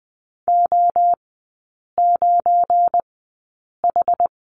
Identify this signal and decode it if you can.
Morse code, heard as a keyed tone, about 20 wpm, spaced as for 8 wpm, 700 Hz. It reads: O9H
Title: 172 Lesson 28e - Number 9 - Three Characters 20wpm